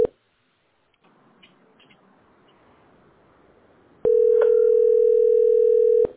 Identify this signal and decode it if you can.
Ringback tone (North America), 440 + 480 Hz.